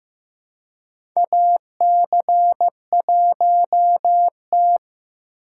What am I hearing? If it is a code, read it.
Morse code, heard as a keyed tone, about 15 wpm, 700 Hz. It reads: AC1T
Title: USA Call Signs - 4 Characters - 4x Character Spacing - 15wpm